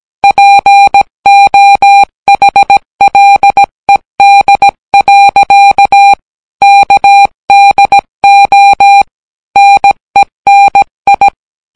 Morse code, heard as a keyed tone, about 17 wpm, 800 Hz. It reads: POHLED. KDO NENI